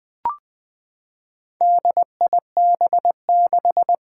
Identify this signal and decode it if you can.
Morse code, heard as a keyed tone, about 21 wpm, 700 Hz. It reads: DIB6
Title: Four Challenging Letter-Number Combinations 20wpm